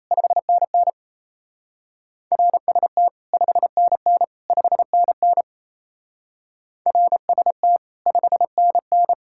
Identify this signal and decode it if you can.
Morse code, heard as a keyed tone, about 34 wpm, 700 Hz. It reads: 5NN RST 5NN 5NN RST 5NN
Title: Speed-Racing Common QSO Phrases Encore - 28wpm